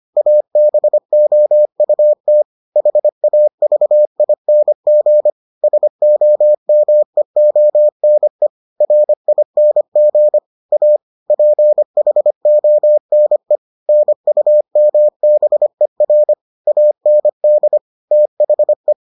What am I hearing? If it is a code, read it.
Morse code, heard as a keyed tone, about 25 wpm, 600 Hz. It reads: ABOUT HAVING SOMEONE RING A PHONE NUMBER AND THE